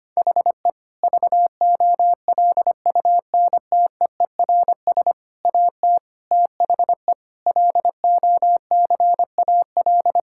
Morse code, heard as a keyed tone, about 25 wpm, 700 Hz. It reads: HE VOLUNTEERS AT THE LOCAL
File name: practice / 25wpm / Difficult Sentences with Hints - 25wpm